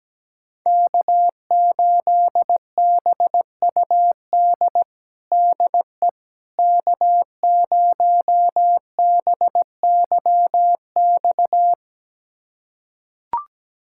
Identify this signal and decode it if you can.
Morse code, heard as a keyed tone, about 17 wpm, 700 Hz. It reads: K8BUD DE K0BYX
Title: USA Callsign DE Callsign - 5 Characters - 17wpm